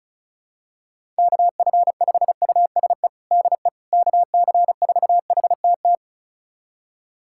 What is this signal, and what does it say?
Morse code, heard as a keyed tone, about 36 wpm, 700 Hz. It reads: KF5USE DE KC4HTT
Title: USA Callsign DE Callsign 35wpm